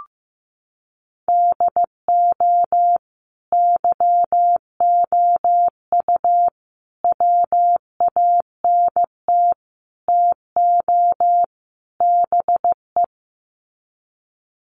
Morse code, heard as a keyed tone, about 15 wpm, 700 Hz. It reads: DO YOU WANT TO BE
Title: Sets of 5 Words Encore2 15wpm